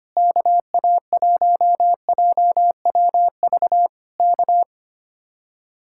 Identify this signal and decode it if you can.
Morse code, heard as a keyed tone, about 24 wpm, 700 Hz. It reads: KA1JWV K
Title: Speed-Racing Common QSO Phrases - 25wpm